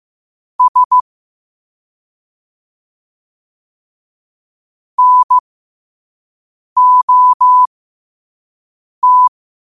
Morse code, heard as a keyed tone, about 15 wpm, spaced as for 5 wpm, 1000 Hz. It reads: S NOT
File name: qotdfiles / 5wpm